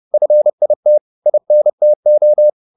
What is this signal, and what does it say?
Morse code, heard as a keyed tone, about 30 wpm, 600 Hz. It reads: FIT INTO